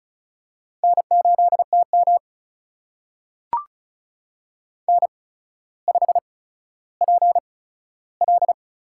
Morse code, heard as a keyed tone, about 36 wpm, spaced as for 15 wpm, 700 Hz. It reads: N8TM N5PL
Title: USA Call Signs - 8x Character Spacing - 35wpm